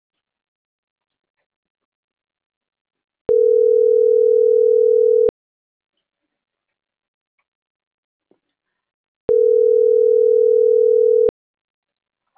Ringback tone (North America), 440 + 480 Hz.